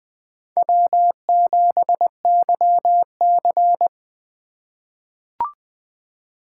Morse code, heard as a keyed tone, about 21 wpm, 700 Hz. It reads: W7YC
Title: USA Call Signs - 2x Character Spacing - 20wpm